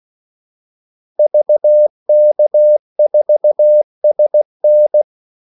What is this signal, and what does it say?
Morse code, heard as a keyed tone, about 16 wpm, 600 Hz. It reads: VK4SN
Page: Kaldesignaler 16 wpm